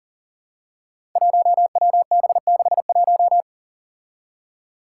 Morse code, heard as a keyed tone, about 40 wpm, 700 Hz. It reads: 1WB61
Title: Five Challenging Letter-Number Combinations 40wpm